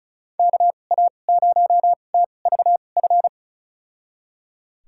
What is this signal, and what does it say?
Morse code, heard as a keyed tone, about 36 wpm, spaced as for 25 wpm, 700 Hz. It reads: KA0TVF
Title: USA Call Signs - 6 Characters - 2x Character Spacing - 35wpm